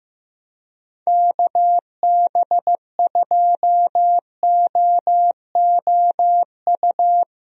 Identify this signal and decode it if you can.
Morse code, heard as a keyed tone, about 15 wpm, 700 Hz. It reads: KB2OOU
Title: USA Call Signs - 6 Characters - Repeated 2x - 15wpm